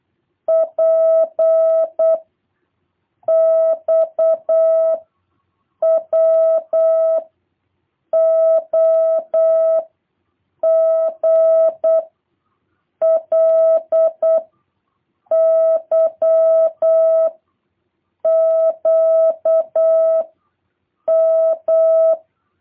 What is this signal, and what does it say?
Morse code, heard as a keyed tone, about 8 wpm, 650 Hz. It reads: P X W O G L Y Q M